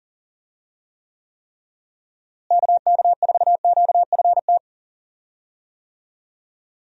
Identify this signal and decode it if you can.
Morse code, heard as a keyed tone, about 40 wpm, 700 Hz. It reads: KK4QFT
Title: USA Call Signs - 6 Characters - Repeated 3x - 40wpm